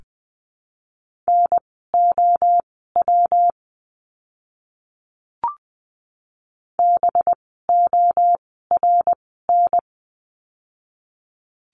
Morse code, heard as a keyed tone, about 20 wpm, spaced as for 14 wpm, 700 Hz. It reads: NOW BORN
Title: 185 Lesson 30f - Letter b - Words 20wpm